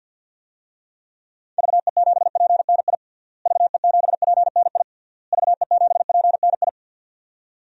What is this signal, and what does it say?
Morse code, heard as a keyed tone, about 52 wpm, 700 Hz. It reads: VE7PNI VE7PNI VE7PNI
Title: DXCC-100 Call Signs - 6 Characters - Repeated 3x - 50wpm